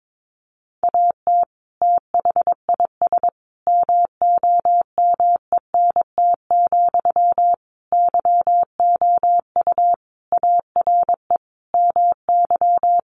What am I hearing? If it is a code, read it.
Morse code, heard as a keyed tone, about 22 wpm, 700 Hz. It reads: AT THIS MOMENT, YOU ARE MY
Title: Sentences from Top 1000 Words 22wpm